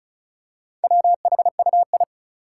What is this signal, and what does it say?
Morse code, heard as a keyed tone, about 36 wpm, 700 Hz. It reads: WHUI
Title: Four Challenging Letter-Number Combinations 35wpm